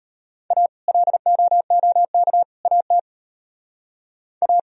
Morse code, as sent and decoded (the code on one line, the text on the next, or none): .- / .-.. --- --- -.- / .- - / .-
A LOOK AT A